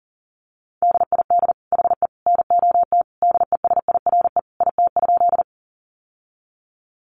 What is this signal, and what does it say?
Morse code, heard as a keyed tone, about 40 wpm, 700 Hz. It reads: DID HE NOT DESIRE IT?
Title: Sentences from Top 1100 Words 40wpm